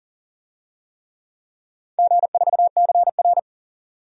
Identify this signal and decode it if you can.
Morse code, heard as a keyed tone, about 40 wpm, 700 Hz. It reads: G4CR
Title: DXCC-100 Call Signs - 4 Characters - 40wpm